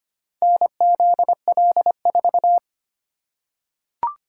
Morse code, heard as a keyed tone, about 25 wpm, 700 Hz. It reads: NZL4
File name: practice / 25wpm / Four Letter-Number Combinations - 25wpm